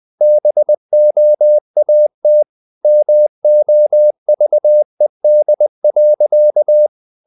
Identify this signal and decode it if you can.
Morse code, heard as a keyed tone, about 20 wpm, 600 Hz. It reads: BOAT MOVED.
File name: the old man and the sea / the old man and the sea part 1 20wpm